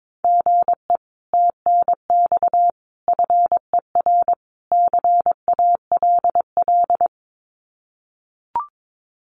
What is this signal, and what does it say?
Morse code, heard as a keyed tone, about 22 wpm, 700 Hz. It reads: GE TNX FER CALL
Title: Common QSO Phrases Encore 22wpm